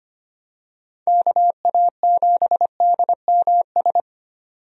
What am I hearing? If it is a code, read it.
Morse code, heard as a keyed tone, about 25 wpm, 700 Hz. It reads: KA7DMS
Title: USA Call Signs - 6 Characters - 25wpm